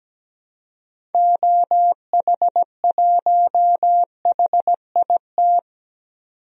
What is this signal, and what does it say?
Morse code, heard as a keyed tone, about 17 wpm, 700 Hz. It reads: OH1HIT
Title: DXCC-100 Call Signs - 6 Characters - 17wpm